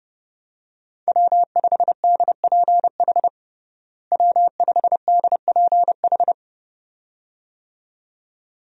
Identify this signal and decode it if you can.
Morse code, heard as a keyed tone, about 31 wpm, 700 Hz. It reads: W5DPH W5DPH
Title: USA Call Signs - 5 Characters - Repeated 2x - 30wpm